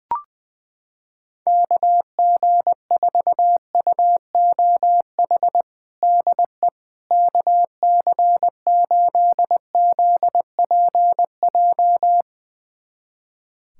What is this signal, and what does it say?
Morse code, heard as a keyed tone, about 21 wpm, 700 Hz. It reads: KG4UOH DE KC8ZPJ
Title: USA Callsign DE Callsign - 6 Characters - 20wpm